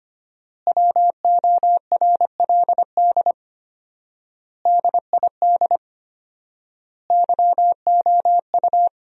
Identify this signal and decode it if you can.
Morse code, heard as a keyed tone, about 25 wpm, 700 Hz. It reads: WORLD DID YOU
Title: 4x Word-Spacing - Sentences from Top 200 Words - 25wpm